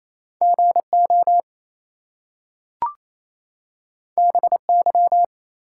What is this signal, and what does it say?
Morse code, heard as a keyed tone, about 27 wpm, 700 Hz. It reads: GO BY
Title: Two Letter Words - 28wpm